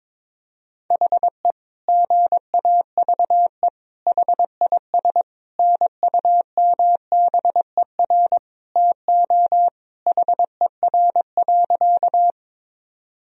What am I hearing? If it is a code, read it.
Morse code, heard as a keyed tone, about 22 wpm, 700 Hz. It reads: HE GAVE HIS NUMBER TO HER.
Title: Sentences from Top 1000 Words - Review Part 1 - 22wpm